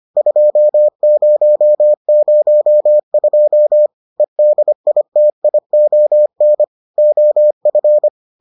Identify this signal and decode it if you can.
Morse code, heard as a keyed tone, about 25 wpm, 600 Hz. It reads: 2002 EDITION OF